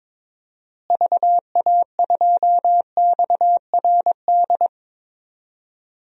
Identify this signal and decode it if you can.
Morse code, heard as a keyed tone, about 21 wpm, 700 Hz. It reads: VA2XRD
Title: DXCC-100 Call Signs - 6 Characters - Repeated 3x - 22wpm